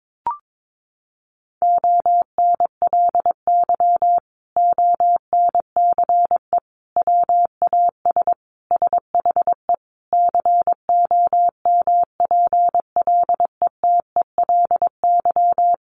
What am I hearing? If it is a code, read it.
Morse code, heard as a keyed tone, about 22 wpm, 700 Hz. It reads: ONLY ONCE WAS SHE COMPLETELY
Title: Sentences from Top 1100 Words 22wpm